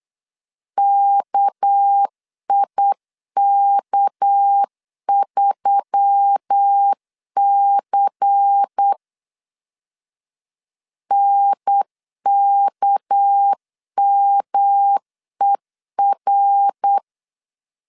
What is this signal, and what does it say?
Morse code, heard as a keyed tone, about 8 wpm, 800 Hz. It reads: KIK3C NKMER
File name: LEZIONI / 6 PKX0 / 6 PK 8wpm 3003